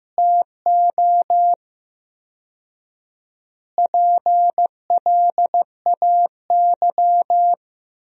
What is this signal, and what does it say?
Morse code, heard as a keyed tone, about 16 wpm, 700 Hz. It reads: TO PLAY
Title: 4x Word-Spacing - Sentences from Top 300 Words - 15wpm